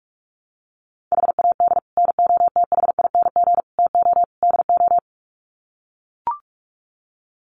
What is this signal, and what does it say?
Morse code, heard as a keyed tone, about 44 wpm, 700 Hz. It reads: HAD NOTHING TO DO
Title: Sets of 4 Words 45wpm